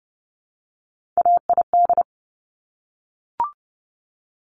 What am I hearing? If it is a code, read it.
Morse code, heard as a keyed tone, about 30 wpm, 700 Hz. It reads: AID
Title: Top Three Letter Combinations in Words 30wpm